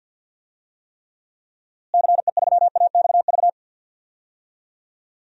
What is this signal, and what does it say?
Morse code, heard as a keyed tone, about 52 wpm, 700 Hz. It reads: CE3AXV
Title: DXCC-100 Call Signs - 6 Characters - Repeated 2x - 50wpm